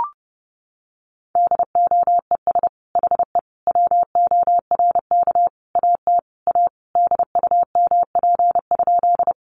Morse code, heard as a keyed tone, about 30 wpm, 700 Hz. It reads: DOES HE WORK AT A DUMP?
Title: Sentences from Top 1300 Words 30wpm